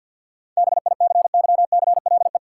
Morse code, heard as a keyed tone, about 52 wpm, 700 Hz. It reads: BICYCLE
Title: Seven Letter Words - 50wpm